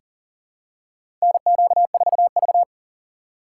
Morse code, heard as a keyed tone, about 40 wpm, 700 Hz. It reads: NQ4V
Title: USA Call Signs - 4 Characters - 2x Character Spacing - 40wpm